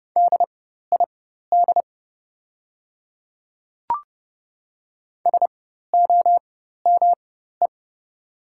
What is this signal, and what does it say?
Morse code, heard as a keyed tone, about 30 wpm, spaced as for 14 wpm, 690 Hz. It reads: DID SOME
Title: 4x Character Spacing - Top 100 Words - 30wpm